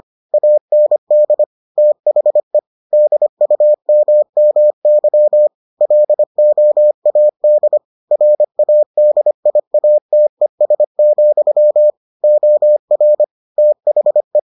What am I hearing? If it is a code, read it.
Morse code, heard as a keyed tone, about 25 wpm, 600 Hz. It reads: AND THE DUMMY LOAD RADIATES, OR THE